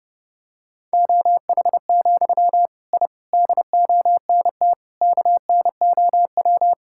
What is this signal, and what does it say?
Morse code, heard as a keyed tone, about 30 wpm, 700 Hz. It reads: OH, I DONT KNOW
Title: Sentences from Top 1000 Words - Review Part 6 - 30wpm